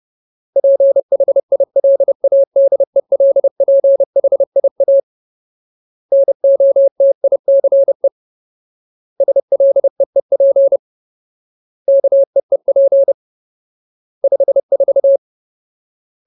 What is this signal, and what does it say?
Morse code, heard as a keyed tone, about 30 wpm, 550 Hz. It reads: PHILADELPHIA NOTICE SLEEP KEEP 54
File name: Ultimate Word List / 30WPM Ultimate